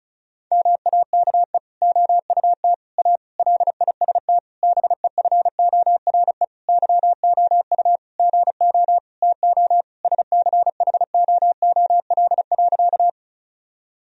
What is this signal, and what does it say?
Morse code, heard as a keyed tone, about 34 wpm, 700 Hz. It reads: MAKE OUT A LIST BEFORE YOU GO TO SCHOOL.
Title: Sentences from Top 1000 Words - Review Part 2 - 35wpm